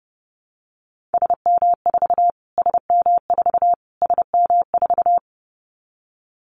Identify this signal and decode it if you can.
Morse code, heard as a keyed tone, about 30 wpm, 700 Hz. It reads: SM4 SM4 SM4
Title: Callsign Prefixes for Top 25 DXCC - Repeated 3x - 30wpm